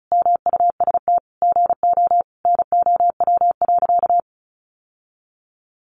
Morse code, heard as a keyed tone, about 34 wpm, 700 Hz. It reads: MUST GO NOW.
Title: Sentences from Top 1300 Words 35wpm